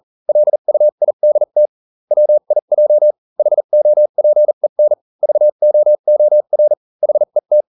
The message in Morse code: LUIDT WIJ HOPEN VOOR HET